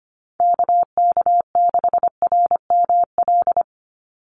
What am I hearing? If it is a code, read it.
Morse code, heard as a keyed tone, about 25 wpm, 700 Hz. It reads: KK6RML
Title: USA Call Signs 25wpm